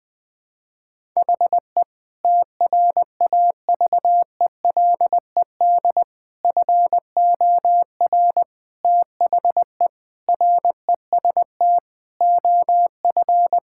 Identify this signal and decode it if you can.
Morse code, heard as a keyed tone, about 19 wpm, 700 Hz. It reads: HE TRAVELED FOR THE REST OF
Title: Sentences with 6-Word Phrases - 20wpm